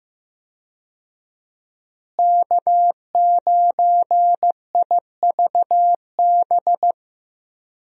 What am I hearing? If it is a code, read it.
Morse code, heard as a keyed tone, about 16 wpm, 700 Hz. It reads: K9IVB
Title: USA Call Signs - 5 Characters - Repeated 3x - 15wpm